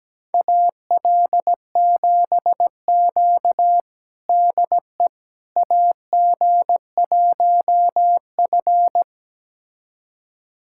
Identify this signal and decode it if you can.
Morse code, heard as a keyed tone, about 17 wpm, 700 Hz. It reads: AL7Q DE AG1F
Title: USA Callsign DE Callsign - 4 Characters - 17wpm